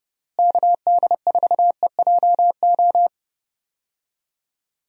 Morse code, heard as a keyed tone, about 30 wpm, 700 Hz. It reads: KD4EJO
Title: USA Call Signs - 6 Characters - Repeated 3x - 30wpm